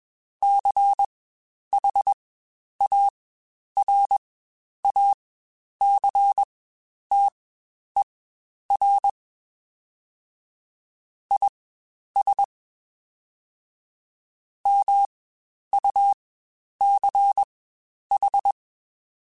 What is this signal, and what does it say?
Morse code, heard as a keyed tone, about 21 wpm, spaced as for 10 wpm, 800 Hz. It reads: CHARACTER IS MUCH
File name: Quotes Spaced / Quotes Spaced-mp3s / Character sp @20